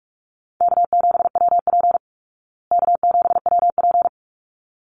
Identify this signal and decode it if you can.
Morse code, heard as a keyed tone, about 44 wpm, 700 Hz. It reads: K7WP K7WP
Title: Easy Call Signs Repeated 2x - 45wpm